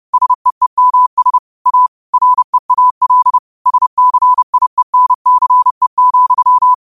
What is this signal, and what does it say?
Morse code, heard as a keyed tone, about 30 wpm, 1000 Hz. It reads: SEEMS A REAL SCIENCE,
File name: qotdfiles / 30wpm